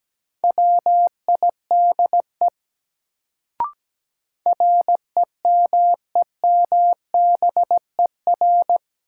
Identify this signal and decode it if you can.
Morse code, heard as a keyed tone, about 17 wpm, 700 Hz. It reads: WIDE REMEMBER
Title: Sets of 4 Words - New Word Drill - 17wpm